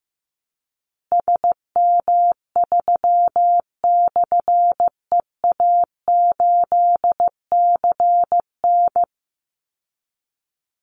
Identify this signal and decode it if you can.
Morse code, heard as a keyed tone, about 15 wpm, 700 Hz. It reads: SM3/EA8CN
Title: Call Signs Repeated 2x - 15wpm